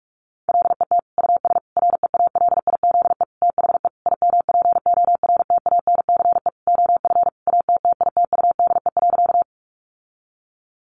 Morse code, heard as a keyed tone, about 44 wpm, 700 Hz. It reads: LET US REALIZE THE IMPORTANCE OF ATTITUDE.
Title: Sentences from Top 1300 Words 45wpm